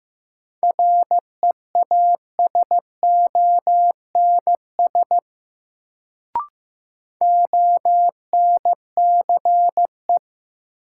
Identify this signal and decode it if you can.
Morse code, heard as a keyed tone, about 15 wpm, 700 Hz. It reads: REASONS ONCE
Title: Sets of 4 Words - New Word Drill - 15wpm